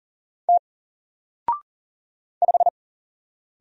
Morse code, heard as a keyed tone, about 40 wpm, 700 Hz. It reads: T 5
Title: Single Letter-Number 40wpm